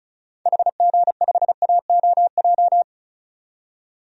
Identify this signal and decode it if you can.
Morse code, heard as a keyed tone, about 35 wpm, 700 Hz. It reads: HG5AOJ
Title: DXCC-100 Call Signs - 6 Characters - 35wpm